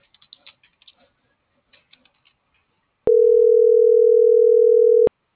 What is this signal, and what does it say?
Ringback tone (North America), 440 + 480 Hz.